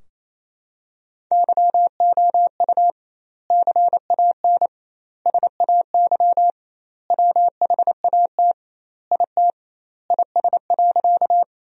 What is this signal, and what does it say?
Morse code, heard as a keyed tone, about 27 wpm, 700 Hz. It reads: YOU CAN SAY WHAT IT IS.
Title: 2x Word-Spacing - Sentences from Top 200 Words - 28wpm